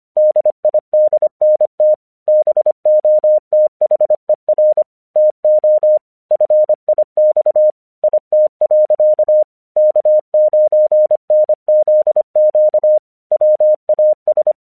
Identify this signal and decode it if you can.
Morse code, heard as a keyed tone, about 26 wpm, 610 Hz. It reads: DIDNT BOTHER TO FIX IT. K9NZQ WAS
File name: 1 English 25wpm / AQC13